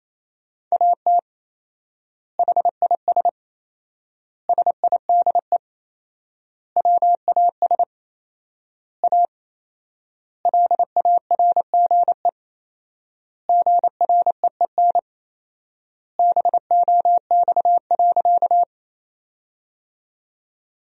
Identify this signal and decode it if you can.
Morse code, heard as a keyed tone, about 28 wpm, 700 Hz. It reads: AT HIS SIDE WAS A LARGE GREEN BOX.
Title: 4x Word-Spacing - Sentences from Top 500 Words - 28wpm